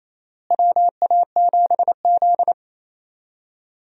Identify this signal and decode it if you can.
Morse code, heard as a keyed tone, about 28 wpm, 700 Hz. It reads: WA7Z
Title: USA Call Signs - 4 Characters - Repeated 2x - 28wpm